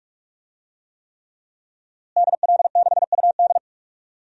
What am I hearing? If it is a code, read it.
Morse code, heard as a keyed tone, about 45 wpm, 700 Hz. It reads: DL6UD
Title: DXCC-100 Call Signs - 5 Characters - Repeated 3x - 45wpm